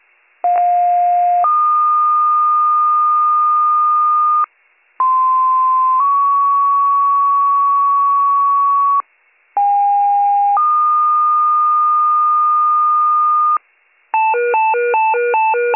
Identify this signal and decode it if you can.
Two-tone page, heard as a one-second tone, then a three-second tone, about 710 then 1220 Hz, about 1040 then 1120 Hz, about 810 then 1220 Hz. It is heard from 0.4 to 13.6 s.